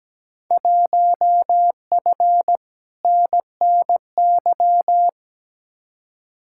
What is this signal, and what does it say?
Morse code, heard as a keyed tone, about 17 wpm, 700 Hz. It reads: 1F NNY
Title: Field Day Exchanges - 17wpm